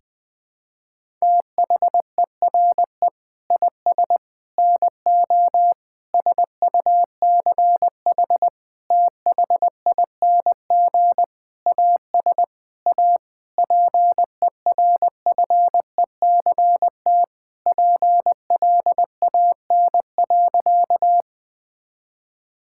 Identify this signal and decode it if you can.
Morse code, heard as a keyed tone, about 21 wpm, 700 Hz. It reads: THERE IS NO SUCH THING AS A PERFECT PLAN.
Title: Sentences with 7-Word Phrases - 20wpm